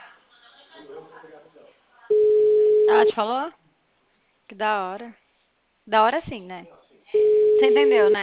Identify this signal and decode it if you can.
Ringback tone (Europe, 425 Hz), with speech.
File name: out- -2005